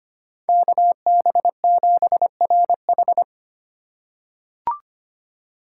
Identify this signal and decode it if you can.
Morse code, heard as a keyed tone, about 25 wpm, 700 Hz. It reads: KB7RH
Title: USA Call Signs - 4x Character Spacing - 25wpm